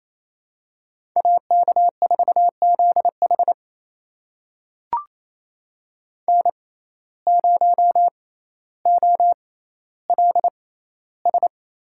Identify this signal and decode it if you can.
Morse code, heard as a keyed tone, about 28 wpm, spaced as for 14 wpm, 700 Hz. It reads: AK4ZH N0OLS
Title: USA Call Signs - 5 Characters - 6x Character Spacing - 28wpm